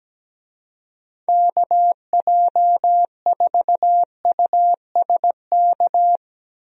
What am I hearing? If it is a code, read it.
Morse code, heard as a keyed tone, about 17 wpm, 700 Hz. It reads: KJ4USK
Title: USA Call Signs - 6 Characters - 17wpm